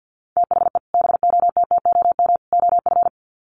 Morse code, heard as a keyed tone, about 50 wpm, 700 Hz. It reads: THE BOTTOM OF
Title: Sets of 3 Words 50wpm